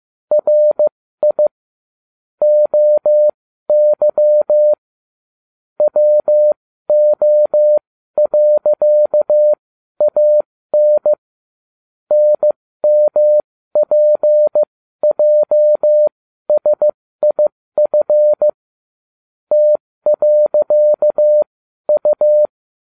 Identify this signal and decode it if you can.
Morse code, heard as a keyed tone, about 15 wpm, spaced as for 12 wpm, 600 Hz. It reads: RI OY WO.AN NMPJSIF T.U
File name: cw / ubacw20 2